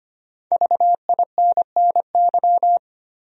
Morse code, heard as a keyed tone, about 25 wpm, 700 Hz. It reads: VINNY
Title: USA Common Names - Max 5 Letters - 25wpm